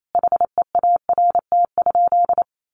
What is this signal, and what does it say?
Morse code, heard as a keyed tone, about 28 wpm, 700 Hz. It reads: HEART?
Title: Sentences from Top 1400 Words 28wpm